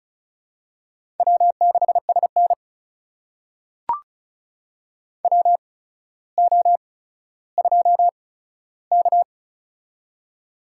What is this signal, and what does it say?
Morse code, heard as a keyed tone, about 36 wpm, spaced as for 15 wpm, 700 Hz. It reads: W6SN WO2K
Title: USA Call Signs - 4 Characters - 8x Character Spacing - 35wpm